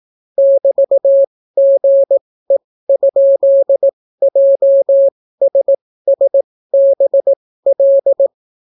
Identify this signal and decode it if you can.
Morse code, heard as a keyed tone, about 18 wpm, spaced as for 14 wpm, 550 Hz. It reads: <BT>GE?JSSBL